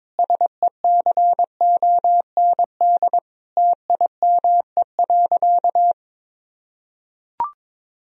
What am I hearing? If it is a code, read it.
Morse code, heard as a keyed tone, about 22 wpm, 700 Hz. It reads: SECOND TIME.
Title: Sentences from Top 1000 Words - Review Part 1 - 22wpm